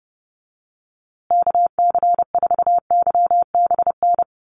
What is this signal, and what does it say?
Morse code, heard as a keyed tone, about 30 wpm, 700 Hz. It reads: KC4YBN
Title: USA Call Signs Repeated 3x 30wpm